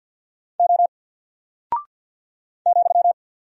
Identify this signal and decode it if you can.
Morse code, heard as a keyed tone, about 49 wpm, 700 Hz. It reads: O ,
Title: Single Character (Letter:Number:Punctuation:Prosign) 50wpm